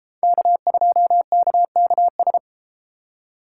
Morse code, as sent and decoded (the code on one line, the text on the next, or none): -.- ..--- -.- -.- ...
K2KKS